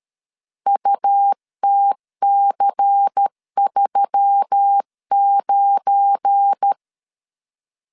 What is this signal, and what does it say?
Morse code, heard as a keyed tone, about 13 wpm, 800 Hz. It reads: UTC39
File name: LEZIONI / 9 28 12wpm 2351